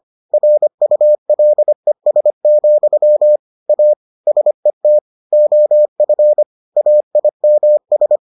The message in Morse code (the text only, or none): RULES, A SET OF AIMS